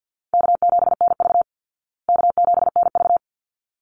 Morse code, heard as a keyed tone, about 50 wpm, 700 Hz. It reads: K7NV K7NV
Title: Easy Call Signs Repeated 2x - 50wpm